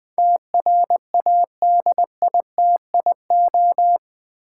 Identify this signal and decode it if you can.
Morse code, heard as a keyed tone, about 20 wpm, 700 Hz. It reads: TRADITIO
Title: Build A Word - 4 Letters At A Time - 11-Letter Words - 20wpm